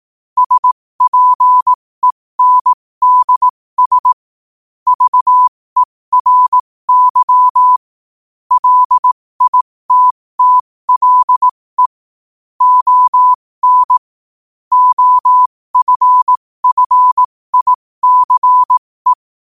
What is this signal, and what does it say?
Morse code, heard as a keyed tone, about 18 wpm, spaced as for 15 wpm, 1000 Hz. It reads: SPENDS VERY LITTLE ON OFFICE
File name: qotdfiles / 15wpm